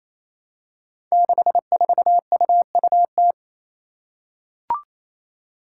Morse code, heard as a keyed tone, about 27 wpm, 700 Hz. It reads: 64UUT
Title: Five Challenging Letter-Number Combinations 28wpm